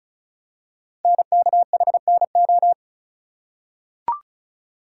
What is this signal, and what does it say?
Morse code, heard as a keyed tone, about 36 wpm, 690 Hz. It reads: NKHNO
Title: Code Groups - 5 Letters - 35wpm